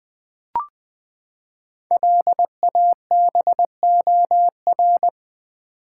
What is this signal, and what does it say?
Morse code, heard as a keyed tone, about 20 wpm, 700 Hz. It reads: LABOR
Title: Build A Word - 5 Letters At A Time - 10-Letter Words - 20wpm